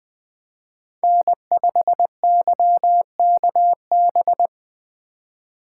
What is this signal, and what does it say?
Morse code, heard as a keyed tone, about 25 wpm, 700 Hz. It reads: N5YKB
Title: Speed Racing - USA Call Signs 20wpm